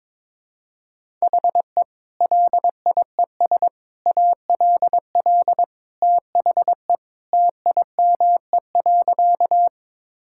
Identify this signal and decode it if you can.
Morse code, heard as a keyed tone, about 22 wpm, 700 Hz. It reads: HE LIES ALL THE TIME.
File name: practice / 22wpm / Sentences with 3-Word Phrases - 22wpm